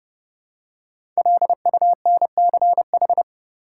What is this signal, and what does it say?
Morse code, heard as a keyed tone, about 32 wpm, 700 Hz. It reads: LUNCH
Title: Five Letter Words - 30wpm